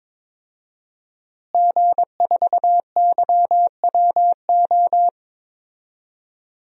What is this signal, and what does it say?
Morse code, heard as a keyed tone, about 21 wpm, 700 Hz. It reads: G4YWO
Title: DXCC-100 Call Signs - 5 Characters - Repeated 3x - 22wpm